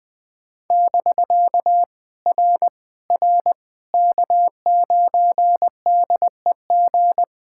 Morse code, heard as a keyed tone, about 20 wpm, 700 Hz. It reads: <BK> R R K9DEG
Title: POTA QSO Practice - 20wpm